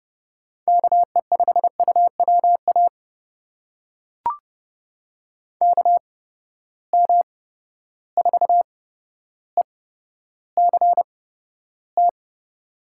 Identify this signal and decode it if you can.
Morse code, heard as a keyed tone, about 30 wpm, spaced as for 13 wpm, 700 Hz. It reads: KE5UWA KM4ECT
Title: USA Call Signs - 6 Characters - 8x Character Spacing - 30wpm